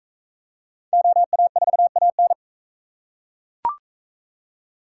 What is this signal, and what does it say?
Morse code, heard as a keyed tone, about 41 wpm, 700 Hz. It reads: OA4AN